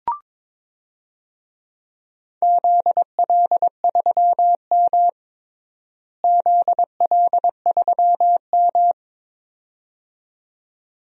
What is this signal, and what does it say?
Morse code, heard as a keyed tone, about 22 wpm, 700 Hz. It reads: ZL3M ZL3M
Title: DXCC-100 Call Signs - 4 Characters - Repeated 2x - 22wpm